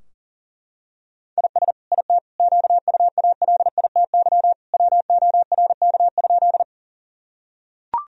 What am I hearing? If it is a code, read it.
Morse code, heard as a keyed tone, about 37 wpm, 700 Hz. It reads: IS IT QUALITY WORK?
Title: Sentences from Top 1000 Words - Review Part 4 - 40wpm